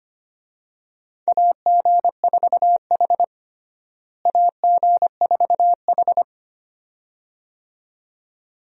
Morse code, heard as a keyed tone, about 25 wpm, 700 Hz. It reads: AG4H AG4H
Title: USA Call Signs - 4 Characters - Repeated 2x - 25wpm